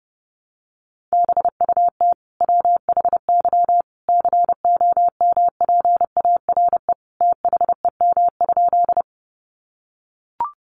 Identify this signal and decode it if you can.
Morse code, heard as a keyed tone, about 30 wpm, 700 Hz. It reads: BUT WHY COMPARE THEM?
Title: Sentences from Top 1300 Words 30wpm